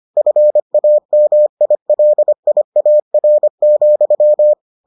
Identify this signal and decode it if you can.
Morse code, heard as a keyed tone, about 25 wpm, 600 Hz. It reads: FAMILIAR,